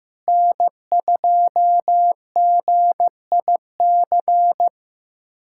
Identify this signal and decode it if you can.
Morse code, heard as a keyed tone, about 15 wpm, 700 Hz. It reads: N2GIC
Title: USA Callsign DE Callsign 15wpm